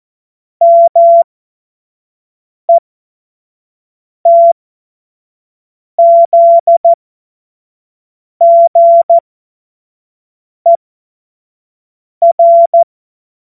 Morse code, heard as a keyed tone, about 14 wpm, spaced as for 5 wpm, 680 Hz. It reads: METZGER
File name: Morsekurs / Witze / Witz 12. Oktober 2021, 25 F70